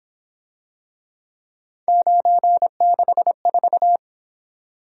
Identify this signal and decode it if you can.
Morse code, heard as a keyed tone, about 26 wpm, 700 Hz. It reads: 964
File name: Speed-Racing - Three Letter-Number Combinations - 22wpm